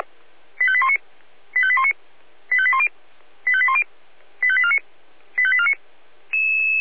Five-tone selective call (ZVEI), heard as quick runs of five tones, tones of about 70 ms. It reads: 85819 85819 85810 85810 85849 85849
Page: /496/SDR/ZVEI/85840/2025.06.25/